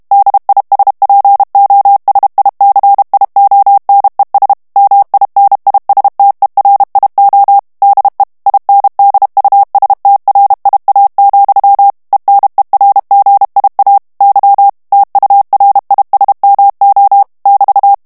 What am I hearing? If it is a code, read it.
Morse code, heard as a keyed tone, about 32 wpm, 800 Hz. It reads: DISPOSICIONES MINISTERIO DE INDUSTRIA, ENERGIA Y TURISMO <BT>
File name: morse mp3 / ESPAGNOL TEXTE EN CLAIR 2 VITESSE 30 MOTS MINUTES